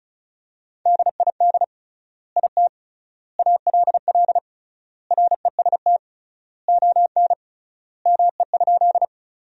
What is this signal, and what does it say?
Morse code, heard as a keyed tone, about 36 wpm, 700 Hz. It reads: DID IT ALL REST ON ME?
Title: 3x Word-Spacing - Sentences from Top 500 Words - 35wpm